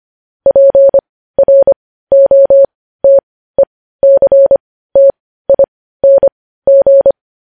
Morse code, heard as a keyed tone, about 25 wpm, spaced as for 15 wpm, 550 Hz. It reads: PROTECTING